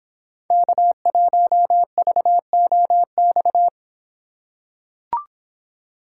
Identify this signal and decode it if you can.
Morse code, heard as a keyed tone, about 24 wpm, 700 Hz. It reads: K1VOX